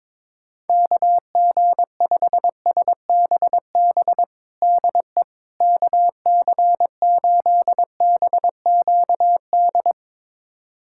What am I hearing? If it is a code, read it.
Morse code, heard as a keyed tone, about 22 wpm, 700 Hz. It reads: KG5SBB DE KC8BQD
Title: USA Callsign DE Callsign - 6 Characters - 22wpm